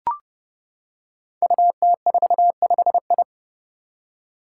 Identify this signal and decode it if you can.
Morse code, heard as a keyed tone, about 32 wpm, 700 Hz. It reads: UT45I
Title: Five Challenging Letter-Number Combinations 30wpm